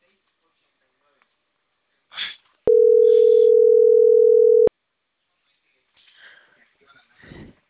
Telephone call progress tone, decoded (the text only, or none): ringback tone